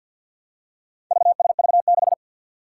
Morse code, heard as a keyed tone, about 47 wpm, 700 Hz. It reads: VSV6
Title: Four Challenging Letter-Number Combinations 50wpm